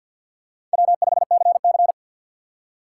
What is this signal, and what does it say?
Morse code, heard as a keyed tone, about 53 wpm, 700 Hz. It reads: W5CC
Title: USA Call Signs - 4 Characters - 2x Character Spacing - 50wpm